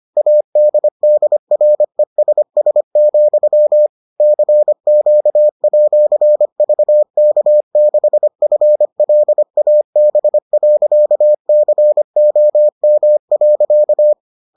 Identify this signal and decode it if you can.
Morse code, heard as a keyed tone, about 25 wpm, 600 Hz. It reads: ADDRESS, CQ@VK6FLAB.COM.